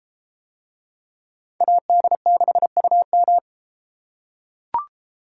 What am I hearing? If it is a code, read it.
Morse code, heard as a keyed tone, about 30 wpm, 700 Hz. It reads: AD6UM